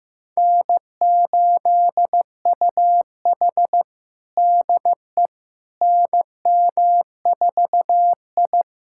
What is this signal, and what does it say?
Morse code, heard as a keyed tone, about 15 wpm, 700 Hz. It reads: N8UH DE NM4I
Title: USA Callsign DE Callsign - 4 Characters - 15wpm